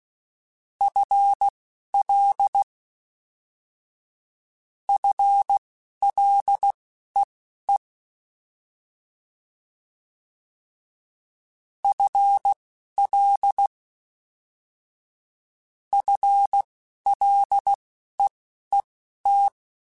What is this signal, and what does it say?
Morse code, heard as a keyed tone, about 16 wpm, spaced as for 11 wpm, 800 Hz. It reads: FL FLEE FL FLEET
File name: FL / FL mp3s / FL Prefixes 1x1 @10